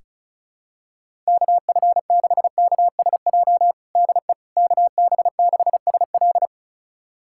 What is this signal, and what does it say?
Morse code, heard as a keyed tone, about 36 wpm, 700 Hz. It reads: KF6KSJ DE KB6SL
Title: USA Callsign DE Callsign 35wpm